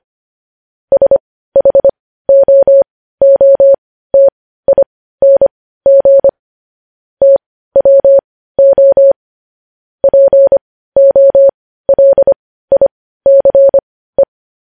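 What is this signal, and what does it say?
Morse code, heard as a keyed tone, about 25 wpm, spaced as for 15 wpm, 550 Hz. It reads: SHOOTING TWO POLICE
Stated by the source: Morse code news headlines at 15 wpm.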